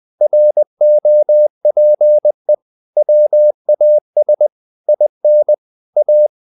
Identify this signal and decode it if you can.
Morse code, heard as a keyed tone, about 20 wpm, 600 Hz. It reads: ROPE WAS IN A
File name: the old man and the sea part 2 20wpm